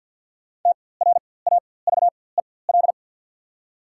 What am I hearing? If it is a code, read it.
Morse code, heard as a keyed tone, about 48 wpm, spaced as for 23 wpm, 690 Hz. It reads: TRAVEL
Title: 4x Character Spacing - Top 500 Words - 50wpm